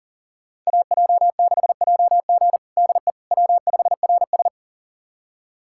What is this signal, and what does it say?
Morse code, heard as a keyed tone, about 40 wpm, 700 Hz. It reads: AJ6JG DE W5RS
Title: USA Callsign DE Callsign 40wpm